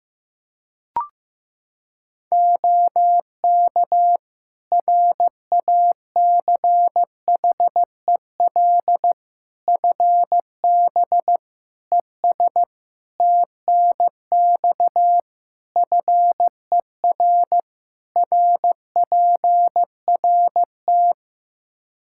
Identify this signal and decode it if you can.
Morse code, heard as a keyed tone, about 15 wpm, 700 Hz. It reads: OK RACHEL FB ES TNX FER RPRT
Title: Practice QSO - Exchange 2 - Info - 15wpm